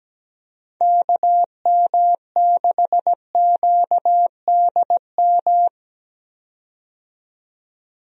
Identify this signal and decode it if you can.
Morse code, heard as a keyed tone, about 17 wpm, 700 Hz. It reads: KM6QDM
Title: USA Call Signs - 6 Characters - 17wpm